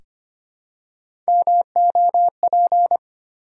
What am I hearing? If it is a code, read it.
Morse code, heard as a keyed tone, about 26 wpm, 700 Hz. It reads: MOP
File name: practice / 25wpm / Easy Words with Hints - 25wpm